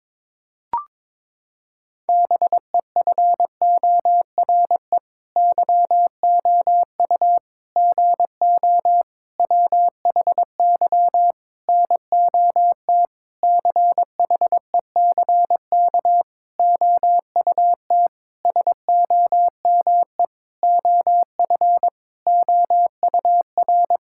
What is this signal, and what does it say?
Morse code, heard as a keyed tone, about 22 wpm, 700 Hz. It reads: BEFORE YOU GO WHY NOT CHECK OUT SOME OF OUR
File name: Sentences with 9-Word Phrases - 22wpm